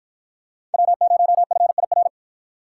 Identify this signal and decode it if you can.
Morse code, heard as a keyed tone, about 52 wpm, 700 Hz. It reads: W0FIR